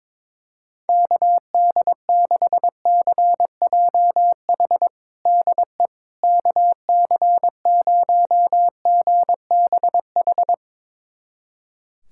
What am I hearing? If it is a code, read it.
Morse code, heard as a keyed tone, about 22 wpm, 700 Hz. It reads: KD6CJH DE KC0GBH
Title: USA Callsign DE Callsign 22wpm